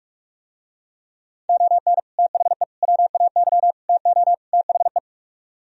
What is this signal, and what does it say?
Morse code, heard as a keyed tone, about 44 wpm, 700 Hz. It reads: ON THE WAY TO THE